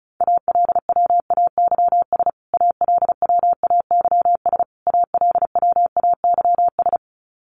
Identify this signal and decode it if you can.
Morse code, heard as a keyed tone, about 34 wpm, 700 Hz. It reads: ALWAYS ALWAYS ALWAYS
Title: Top 500 Words Repeated 35wpm